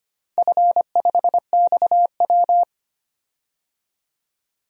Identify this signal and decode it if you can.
Morse code, heard as a keyed tone, about 25 wpm, 700 Hz. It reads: F5XW
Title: DXCC-100 Call Signs - 4 Characters - 25wpm